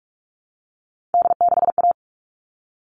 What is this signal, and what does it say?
Morse code, heard as a keyed tone, about 44 wpm, 700 Hz. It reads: D6A
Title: All Cut Numbers - 3-Digit 45wpm